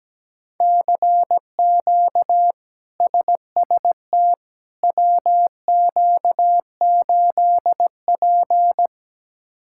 Morse code, heard as a keyed tone, about 17 wpm, 700 Hz. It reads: CQ SST WQ8P
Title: K1USN SST QSO Practice - 17wpm